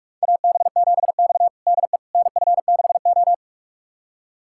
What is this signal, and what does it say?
Morse code, heard as a keyed tone, about 44 wpm, 700 Hz. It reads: AB7X DE NF6O
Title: USA Callsign DE Callsign - 4 Characters - 45wpm